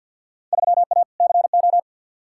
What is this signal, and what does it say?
Morse code, heard as a keyed tone, about 48 wpm, 700 Hz. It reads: 3A CO
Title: Field Day Exchanges Encore - 50wpm